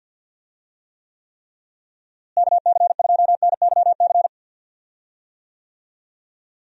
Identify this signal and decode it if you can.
Morse code, heard as a keyed tone, about 50 wpm, 700 Hz. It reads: KC2NYC
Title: USA Call Signs - 6 Characters - Repeated 3x - 50wpm